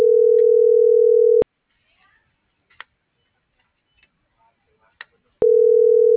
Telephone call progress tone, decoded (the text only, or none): ringback tone